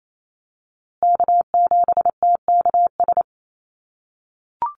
Morse code, heard as a keyed tone, about 28 wpm, 700 Hz. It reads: K7TKS
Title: USA Call Signs Repeated 3x 28wpm